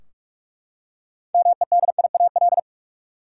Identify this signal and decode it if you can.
Morse code, heard as a keyed tone, about 42 wpm, 700 Hz. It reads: MEDIAL
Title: Practice QSO - Exchange 3 - More Info - 45wpm